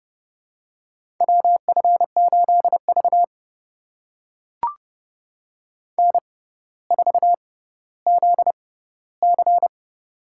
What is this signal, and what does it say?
Morse code, heard as a keyed tone, about 31 wpm, spaced as for 15 wpm, 700 Hz. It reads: WF8V N4ZC
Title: USA Call Signs - 4 Characters - 6x Character Spacing - 30wpm